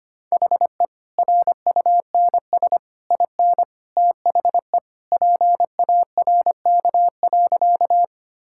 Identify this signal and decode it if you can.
Morse code, heard as a keyed tone, about 25 wpm, 690 Hz. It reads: HE RUNS IN THE PARK.
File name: Easy Sentences with Hints - 25wpm